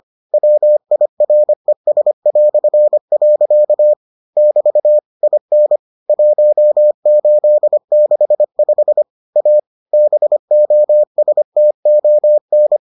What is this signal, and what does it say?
Morse code, heard as a keyed tone, about 25 wpm, 600 Hz. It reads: WIRES". <BT> IN 1865 A BOSTON